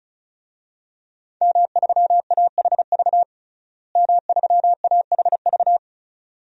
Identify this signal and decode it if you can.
Morse code, heard as a keyed tone, about 34 wpm, 700 Hz. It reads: M3AHV M3AHV
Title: DXCC-100 Call Signs - 5 Characters - Repeated 2x - 35wpm